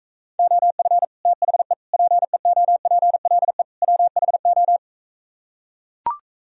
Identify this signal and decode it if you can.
Morse code, heard as a keyed tone, about 44 wpm, 700 Hz. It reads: OF THE PEOPLE WHO